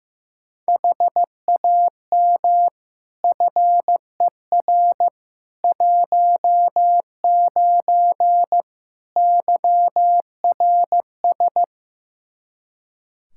Morse code, heard as a keyed tone, about 15 wpm, 700 Hz. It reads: HAM FER 19 YRS
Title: K1USN SST QSO Practice 15wpm